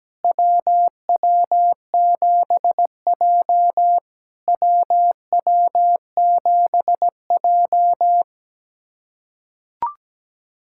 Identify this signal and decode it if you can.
Morse code, heard as a keyed tone, about 17 wpm, 700 Hz. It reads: WW7J WW7J
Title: SOTA QSO Practice 17wpm